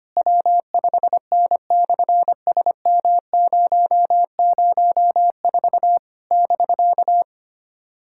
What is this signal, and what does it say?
Morse code, heard as a keyed tone, about 24 wpm, 700 Hz. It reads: W5N/SM004 <BK>
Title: SOTA QSO Practice 25wpm